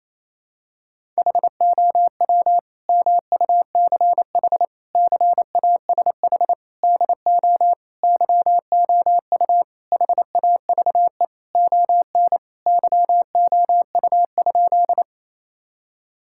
Morse code, heard as a keyed tone, about 28 wpm, 700 Hz. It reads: HOW MUCH CASH DO YOU HAVE ON YOU?
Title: Sentences from Top 1000 Words - Review Part 5 - 28wpm